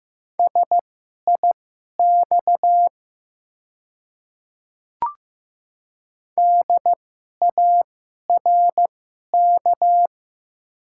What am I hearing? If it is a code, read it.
Morse code, heard as a keyed tone, about 15 wpm, spaced as for 11 wpm, 700 Hz. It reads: SIX DARK
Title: 2x Character Spacing - Top 500 Words - 15wpm